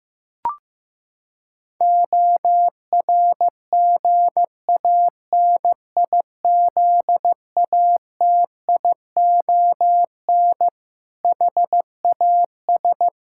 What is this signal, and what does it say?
Morse code, heard as a keyed tone, about 16 wpm, 700 Hz. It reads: ORGANIZATION HAS
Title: Sentences from Top 1000 Words - Review Part 5 - 15wpm